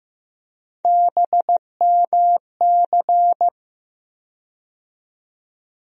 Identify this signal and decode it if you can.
Morse code, heard as a keyed tone, about 15 wpm, 700 Hz. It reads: BMC
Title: USA Call Sign Postfixes - 3 Letters 15wpm